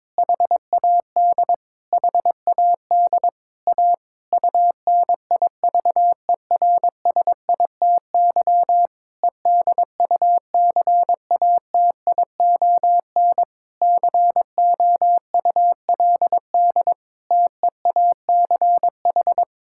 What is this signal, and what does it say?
Morse code, heard as a keyed tone, about 22 wpm, 700 Hz. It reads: HAD HAD A UNIVERSITY EDUCATION COULD TEACH
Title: Sentences from Top 1000 Words - Review Part 6 - 22wpm